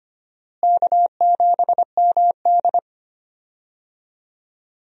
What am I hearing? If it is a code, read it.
Morse code, heard as a keyed tone, about 25 wpm, 700 Hz. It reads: K7MD
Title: USA Call Signs - 4 Characters - Repeated 2x - 25wpm